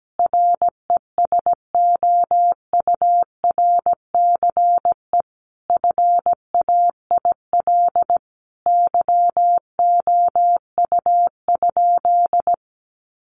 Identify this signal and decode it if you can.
Morse code, heard as a keyed tone, about 17 wpm, 700 Hz. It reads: RESOURCE FAIL YOU?
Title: Sentences from Top 1300 Words 17wpm